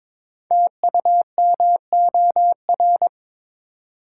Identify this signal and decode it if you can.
Morse code, heard as a keyed tone, about 23 wpm, 700 Hz. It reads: TUMOR
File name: 5 Letter Words - First 3 Letters Given - 22wpm